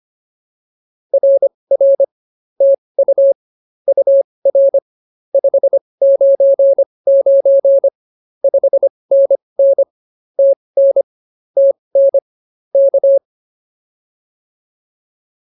Morse code, heard as a keyed tone, about 25 wpm, spaced as for 20 wpm, 590 Hz. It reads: RR TU UR 599 5NN TN TN K